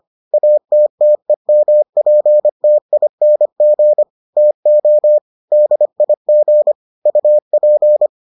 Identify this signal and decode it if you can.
Morse code, heard as a keyed tone, about 25 wpm, 600 Hz. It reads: ATTEMPTING TO DIG UP